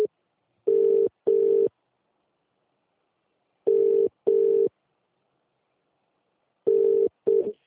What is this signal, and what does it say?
Ringback tone (UK), 400 + 450 Hz.